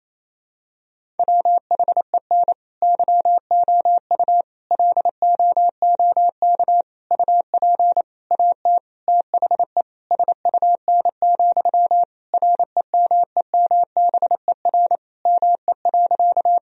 Morse code, heard as a keyed tone, about 28 wpm, 700 Hz. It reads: WHEN YOU LOOK UP AT THE SUN, REMEMBER ME.
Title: Sentences from Top 1000 Words - Review Part 1 - 28wpm